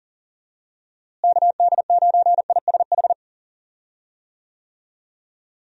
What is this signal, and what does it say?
Morse code, heard as a keyed tone, about 40 wpm, 700 Hz. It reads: KD9ISH
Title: USA Call Signs - 6 Characters - Repeated 3x - 40wpm